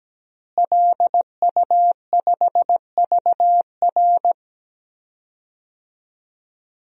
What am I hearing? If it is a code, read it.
Morse code, heard as a keyed tone, about 17 wpm, 700 Hz. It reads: LU5VR
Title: DXCC-100 Call Signs - 5 Characters - Repeated 3x - 17wpm